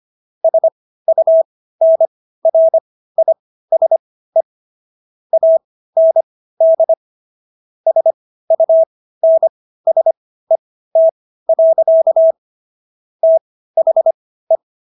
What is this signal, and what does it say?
Morse code, heard as a keyed tone, about 25 wpm, spaced as for 15 wpm, 650 Hz. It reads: SUNRISE AND SUNSET. THE
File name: SS110 15